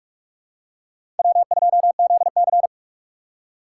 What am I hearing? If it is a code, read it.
Morse code, heard as a keyed tone, about 42 wpm, 700 Hz. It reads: W2ZC
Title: USA Call Signs - 4 Characters - 4x Character Spacing - 45wpm